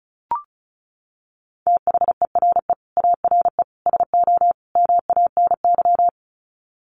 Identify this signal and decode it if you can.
Morse code, heard as a keyed tone, about 34 wpm, 700 Hz. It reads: THERE ARE SO MANY
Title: Sets of 4 Words 35wpm